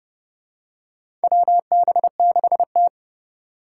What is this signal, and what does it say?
Morse code, heard as a keyed tone, about 31 wpm, 700 Hz. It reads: WB6T
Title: USA Call Signs - 3x Character Spacing - 30wpm